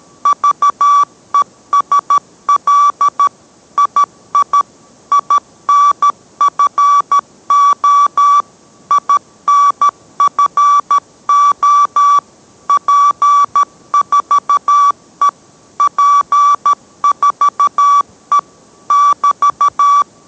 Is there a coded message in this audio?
Yes: Morse code from 0.3 to 20.0 s, heard as a keyed tone, about 13 wpm, 1150 Hz. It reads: VESL II INFO INFO P4E P4E <BT>